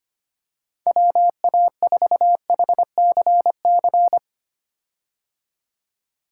Morse code, heard as a keyed tone, about 25 wpm, 700 Hz. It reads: WA4HCC
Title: K1USN SST QSO Practice - 25wpm